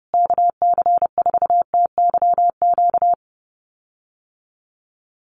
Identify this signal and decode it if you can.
Morse code, heard as a keyed tone, about 30 wpm, 700 Hz. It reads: KC4TYQ
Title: USA Call Signs Repeated 3x 30wpm